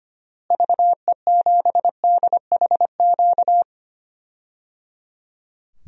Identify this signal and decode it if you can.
Morse code, heard as a keyed tone, about 25 wpm, 700 Hz. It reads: VE7DHQ
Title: DXCC-100 Call Signs - 6 Characters - Repeated 3x - 25wpm